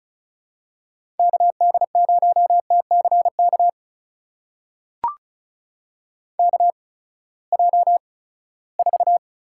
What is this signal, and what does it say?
Morse code, heard as a keyed tone, about 34 wpm, spaced as for 20 wpm, 700 Hz. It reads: KD0TCK KJ4
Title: USA Call Signs - 6 Characters - 8x Character Spacing - 35wpm